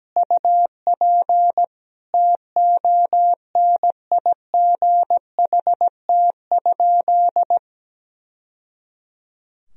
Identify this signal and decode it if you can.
Morse code, heard as a keyed tone, about 17 wpm, 700 Hz. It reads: UP TONIGHT?
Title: Sentences from Top 1000 Words - Review Part 3 - 17wpm